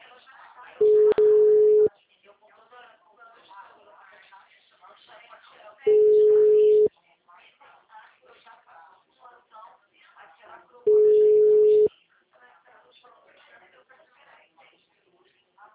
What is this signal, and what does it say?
Ringback tone (Europe, 425 Hz), with speech.